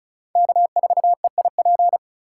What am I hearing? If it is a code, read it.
Morse code, heard as a keyed tone, about 36 wpm, 700 Hz. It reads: K4EIP
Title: USA Call Signs - 5 Characters - 2x Character Spacing - 35wpm